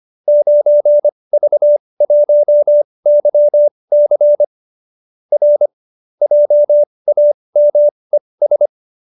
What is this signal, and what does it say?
Morse code, heard as a keyed tone, about 25 wpm, spaced as for 20 wpm, 590 Hz. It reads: 9V1YC R JAMES
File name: qso207b 20